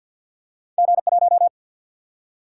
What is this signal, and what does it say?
Morse code, heard as a keyed tone, about 53 wpm, 700 Hz. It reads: G 1
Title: Single-Letter Single-Number 50wpm